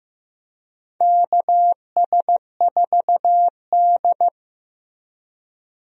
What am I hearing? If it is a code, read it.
Morse code, heard as a keyed tone, about 15 wpm, 700 Hz. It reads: KS4D
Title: USA Call Signs - 4 Characters - Repeated 3x - 15wpm